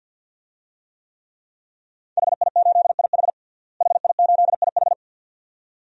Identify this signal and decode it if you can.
Morse code, heard as a keyed tone, about 49 wpm, 700 Hz. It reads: HI8IH HI8IH
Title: DXCC-100 Call Signs - 5 Characters - Repeated 2x - 50wpm